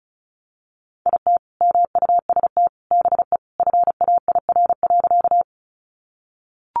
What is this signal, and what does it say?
Morse code, heard as a keyed tone, about 34 wpm, 700 Hz. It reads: IT MUST BE FAIR.
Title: Sentences from Top 1300 Words 35wpm